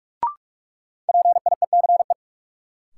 Morse code, heard as a keyed tone, about 45 wpm, 700 Hz. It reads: PIECE
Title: Sets of 3 Words - New Word Drill - 45wpm